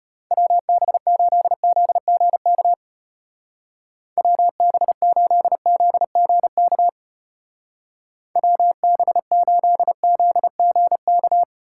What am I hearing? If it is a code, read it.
Morse code, heard as a keyed tone, about 31 wpm, 700 Hz. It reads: WB8ZGK WB8ZGK WB8ZGK